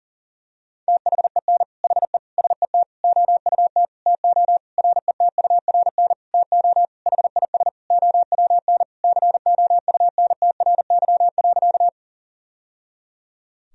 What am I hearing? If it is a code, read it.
Morse code, heard as a keyed tone, about 40 wpm, 700 Hz. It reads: THEN HE SET OUT TO RETURN TO HIS OWN COUNTRY.
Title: Sentences from Top 1000 Words - Review Part 1 - 40wpm